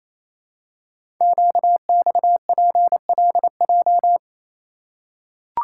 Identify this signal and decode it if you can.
Morse code, heard as a keyed tone, about 28 wpm, 700 Hz. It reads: QXPLJ
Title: Code Groups - 5 Letters - 28wpm